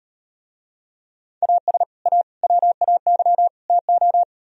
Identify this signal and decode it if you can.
Morse code, heard as a keyed tone, about 37 wpm, 700 Hz. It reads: AS A WAY TO